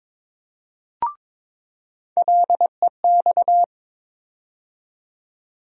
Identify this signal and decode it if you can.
Morse code, heard as a keyed tone, about 21 wpm, 700 Hz. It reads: LEX
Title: USA Call Sign Postfixes - 3 Letters 22wpm